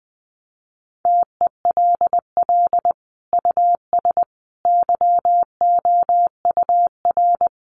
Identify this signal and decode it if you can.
Morse code, heard as a keyed tone, about 20 wpm, 700 Hz. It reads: TELL US YOUR
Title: Sentences from Top 1200 Words 20wpm